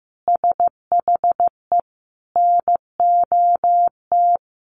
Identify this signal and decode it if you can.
Morse code, heard as a keyed tone, about 15 wpm, 700 Hz. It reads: SHE NOT
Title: Sentences from Top 1400 Words 15wpm